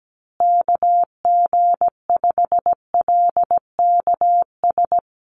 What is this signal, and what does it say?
Morse code, heard as a keyed tone, about 17 wpm, 700 Hz. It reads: KG5LKS
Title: USA Call Signs Repeated 3x 17wpm